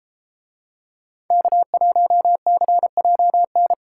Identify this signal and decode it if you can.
Morse code, heard as a keyed tone, about 31 wpm, 700 Hz. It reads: K1CJN